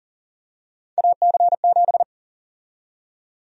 Morse code, heard as a keyed tone, about 40 wpm, 700 Hz. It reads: AC7
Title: USA Call Sign Prefixes - 40wpm